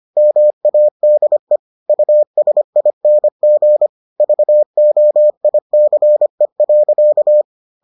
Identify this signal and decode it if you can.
Morse code, heard as a keyed tone, about 25 wpm, 600 Hz. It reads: MADE USING VOICE.